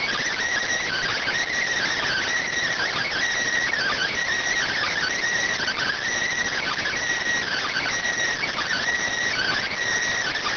Emergency Alert System (SAME) tones (header bursts), heard as harsh buzzing bursts. It is heard from 0.7 to 9.4 s.